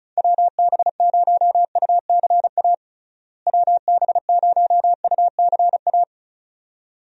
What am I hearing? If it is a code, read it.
Morse code, heard as a keyed tone, about 34 wpm, 700 Hz. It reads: WB0UCA WB0UCA
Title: USA Call Signs - 6 Characters - Repeated 2x - 35wpm